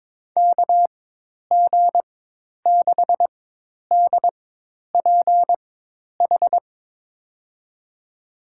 Morse code, heard as a keyed tone, about 22 wpm, spaced as for 10 wpm, 700 Hz. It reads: KG6DPH
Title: USA Call Signs - 6 Characters - 4x Character Spacing - 22wpm